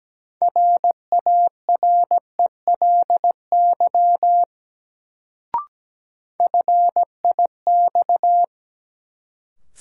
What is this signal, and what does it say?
Morse code, heard as a keyed tone, about 17 wpm, 700 Hz. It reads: RARELY FIX
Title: Sentences from Top 1400 Words - New Word Drill Alt - 17wpm